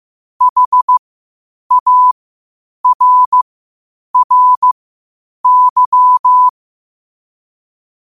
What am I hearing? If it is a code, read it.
Morse code, heard as a keyed tone, about 15 wpm, spaced as for 8 wpm, 1000 Hz. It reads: HARRY
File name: qotdfiles / 8wpm